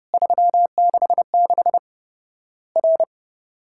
Morse code, heard as a keyed tone, about 30 wpm, 700 Hz. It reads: 366 R
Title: Worked All Europe DX Contest - QTC Exchange - 30wpm